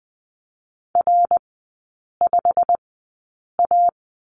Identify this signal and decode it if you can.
Morse code, heard as a keyed tone, about 20 wpm, spaced as for 8 wpm, 690 Hz. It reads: R5A
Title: 087 Lesson 15e - Number 5 - Three Characters 20wpm